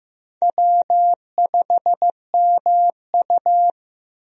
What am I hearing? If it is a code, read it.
Morse code, heard as a keyed tone, about 15 wpm, 700 Hz. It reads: W5MU
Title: USA Call Signs - 4 Characters - 15wpm